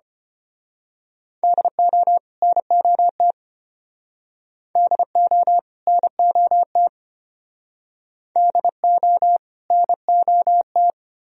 Morse code, heard as a keyed tone, about 31 wpm, 700 Hz. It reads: DO NOT DO NOT DO NOT
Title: Speed-Racing - Sets of 2 Words 25wpm